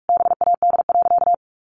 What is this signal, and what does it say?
Morse code, heard as a keyed tone, about 44 wpm, 700 Hz. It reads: BAD.
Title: Sentences from Top 900 Words 45wpm